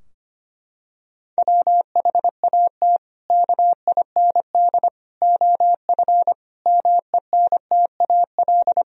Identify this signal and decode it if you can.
Morse code, heard as a keyed tone, about 25 wpm, 700 Hz. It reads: WHAT KIND OF MENTAL
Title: Sentences from Top 1000 Words - Review Part 4 - 25wpm